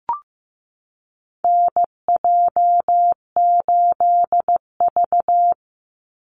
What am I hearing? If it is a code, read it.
Morse code, heard as a keyed tone, about 15 wpm, 700 Hz. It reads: NJ8V
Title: Call Signs Repeated 2x - 15wpm